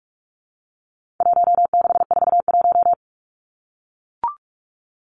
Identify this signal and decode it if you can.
Morse code, heard as a keyed tone, about 44 wpm, 700 Hz. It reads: 1641
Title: Numbers - Contest Style - 4-Digit 45wpm